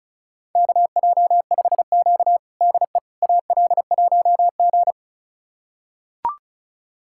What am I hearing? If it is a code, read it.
Morse code, heard as a keyed tone, about 36 wpm, 700 Hz. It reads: KJ5Q DE AL1G
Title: USA Callsign DE Callsign - 4 Characters - 35wpm